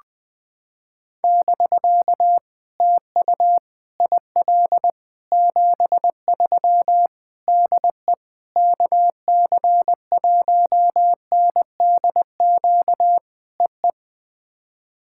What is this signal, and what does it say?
Morse code, heard as a keyed tone, about 21 wpm, 700 Hz. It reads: <BK> TU IL 73 DE KC1NDQ EE
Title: POTA QSO Practice - 20wpm